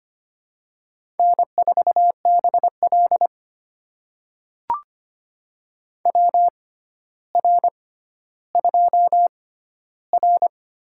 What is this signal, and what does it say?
Morse code, heard as a keyed tone, about 25 wpm, spaced as for 13 wpm, 700 Hz. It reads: N4BL WR2R
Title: USA Call Signs - 4 Characters - 6x Character Spacing - 25wpm